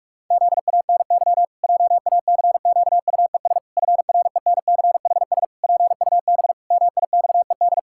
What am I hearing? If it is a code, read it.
Morse code, heard as a keyed tone, about 46 wpm, 700 Hz. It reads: ZANY JACQUES FRENCHS PUB MIXED
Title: Pangram Challenge v2 - 45wpm